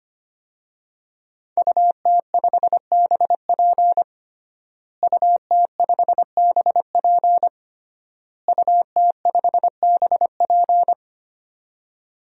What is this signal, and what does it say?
Morse code, heard as a keyed tone, about 25 wpm, 700 Hz. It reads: UT5BP UT5BP UT5BP
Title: DXCC-100 Call Signs - 5 Characters - Repeated 3x - 25wpm